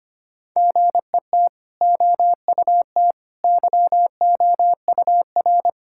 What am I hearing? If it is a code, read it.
Morse code, heard as a keyed tone, about 25 wpm, 690 Hz. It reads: GET OUT YOUR
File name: Easy Sentences with Hints - 25wpm